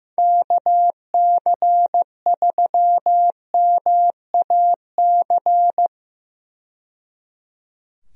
Morse code, heard as a keyed tone, about 15 wpm, 700 Hz. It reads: KC3MAC
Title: USA Call Signs - 6 Characters - 15wpm